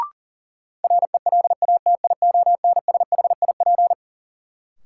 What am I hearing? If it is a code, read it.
Morse code, heard as a keyed tone, about 37 wpm, 700 Hz. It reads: RELATIONSHIP
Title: Sentences from Top 1100 Words - New Word Drill Alt - 40wpm